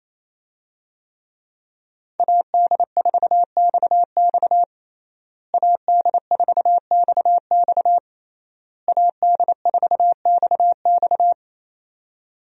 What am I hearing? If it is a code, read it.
Morse code, heard as a keyed tone, about 28 wpm, 700 Hz. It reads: AD4XX AD4XX AD4XX
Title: USA Call Signs - 5 Characters - Repeated 3x - 28wpm